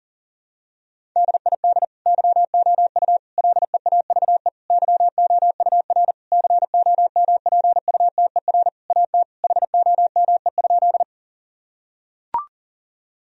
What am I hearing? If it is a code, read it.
Morse code, heard as a keyed tone, about 40 wpm, 700 Hz. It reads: DID YOU LEAVE YOUR COMPUTER AT HOME?
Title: Sentences from Top 1000 Words - Review Part 3 - 40wpm